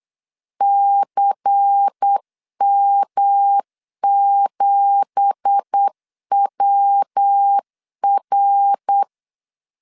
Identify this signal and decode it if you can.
Morse code, heard as a keyed tone, about 8 wpm, 800 Hz. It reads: CM7WR
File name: LEZIONI / 9 2867 8wpm 3440